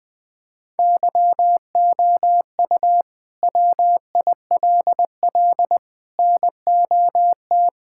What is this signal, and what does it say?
Morse code, heard as a keyed tone, about 20 wpm, 700 Hz. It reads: YOU WILL NOT
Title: Sentences from Top 1600 Words 20wpm